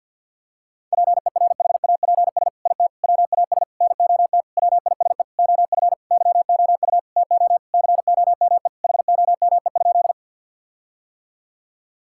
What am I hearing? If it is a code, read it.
Morse code, heard as a keyed tone, about 51 wpm, 700 Hz. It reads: PERHAPS IT WAS NOT WISE OF YOU TO COME HOME?
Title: Sentences from Top 1600 Words 50wpm